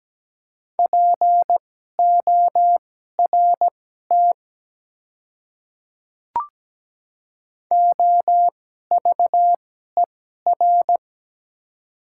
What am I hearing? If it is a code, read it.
Morse code, heard as a keyed tone, about 17 wpm, spaced as for 12 wpm, 690 Hz. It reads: PORT OVER
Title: 2x Character Spacing - Top 300 Words - 17wpm